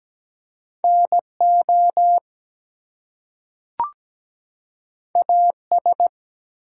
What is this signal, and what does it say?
Morse code, heard as a keyed tone, about 17 wpm, 700 Hz. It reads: NO AS
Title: Two Letter Words - 17wpm